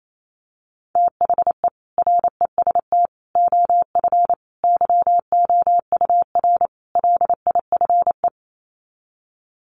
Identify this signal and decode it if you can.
Morse code, heard as a keyed tone, about 28 wpm, 700 Hz. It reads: THE REST OF YOUR LIFE
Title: Sets of 5 Words 28wpm